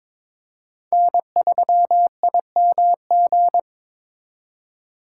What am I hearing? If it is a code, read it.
Morse code, heard as a keyed tone, about 26 wpm, 700 Hz. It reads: N3IMG
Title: Speed Racing - USA 1x3 Call Signs 22wpm